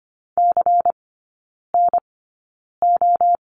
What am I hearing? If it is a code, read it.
Morse code, heard as a keyed tone, about 25 wpm, spaced as for 9 wpm, 700 Hz. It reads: CNO
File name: course / 25:10wpm / 093 Lesson 16e - Letter c - Three Characters 25wpm